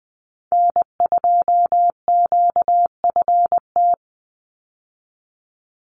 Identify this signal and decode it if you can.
Morse code, heard as a keyed tone, about 20 wpm, 700 Hz. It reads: N2QFT
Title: USA Call Signs 20wpm